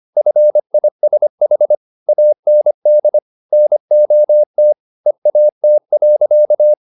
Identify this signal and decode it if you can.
Morse code, heard as a keyed tone, about 25 wpm, 600 Hz. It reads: FISH AND NOT EAT.
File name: the old man and the sea part 1 25wpm